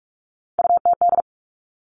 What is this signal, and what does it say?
Morse code, heard as a keyed tone, about 46 wpm, 700 Hz. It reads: UTD
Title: Three Challenging Letter-Number Combinations 45wpm